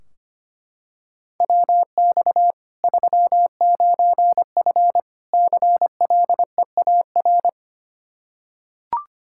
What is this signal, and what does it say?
Morse code, heard as a keyed tone, about 25 wpm, 700 Hz. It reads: WX 39F CLEAR
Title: Practice QSO - Exchange 2 - Info - 25wpm